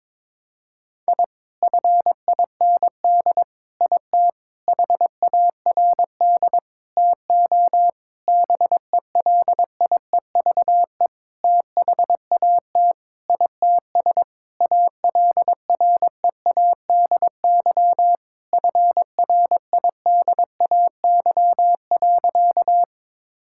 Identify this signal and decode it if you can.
Morse code, heard as a keyed tone, about 21 wpm, 700 Hz. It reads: I FIND IT HARD TO BELIEVE THAT ITS ALREADY FRIDAY.
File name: Sentences with 7-Word Phrases - 22wpm